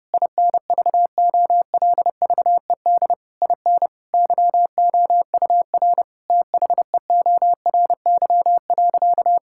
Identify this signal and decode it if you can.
Morse code, heard as a keyed tone, about 30 wpm, 700 Hz. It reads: INVOLVED IN YOUR THEORY.
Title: Sentences from Top 1000 Words - Review Part 5 - 30wpm